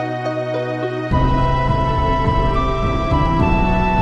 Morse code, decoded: none